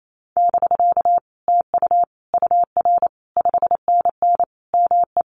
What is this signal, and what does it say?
Morse code, heard as a keyed tone, about 28 wpm, 700 Hz. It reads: <BK> TU UR 5NN ME
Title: Common QSO Phrases Encore 28wpm